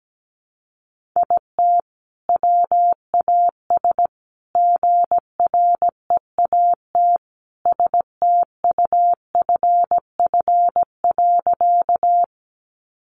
Morse code, heard as a keyed tone, about 17 wpm, 700 Hz. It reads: IT WAS GREAT STUFF.
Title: Sentences from Top 1400 Words 17wpm